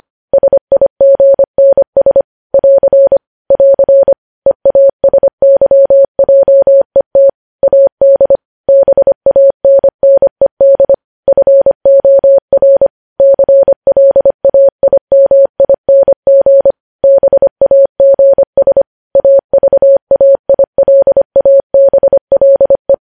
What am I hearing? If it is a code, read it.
Morse code, heard as a keyed tone, about 25 wpm, 550 Hz. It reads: SIGNS <AR> <AR> EASYJET AD BANNED FOR CLAIMING BAGS AVAILABLE
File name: podcast / News headlines-1x-25x25